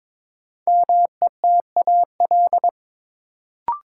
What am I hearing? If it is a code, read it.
Morse code, heard as a keyed tone, about 22 wpm, 700 Hz. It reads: METAL
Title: Sentences from Top 1600 Words - New Word Drill Alt - 22wpm